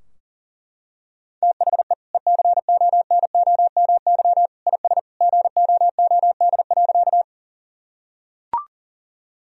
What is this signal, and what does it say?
Morse code, heard as a keyed tone, about 40 wpm, 700 Hz. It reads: THE ECONOMY IS GOOD.
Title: Sentences from Top 1000 Words - Review Part 3 - 40wpm